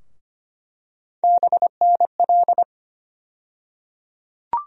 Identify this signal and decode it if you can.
Morse code, heard as a keyed tone, about 25 wpm, 700 Hz. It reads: BNL
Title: USA Call Sign Postfixes - 3 Letters 25wpm